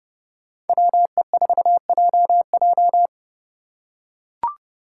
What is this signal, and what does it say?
Morse code, heard as a keyed tone, about 30 wpm, 700 Hz. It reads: WE4JJ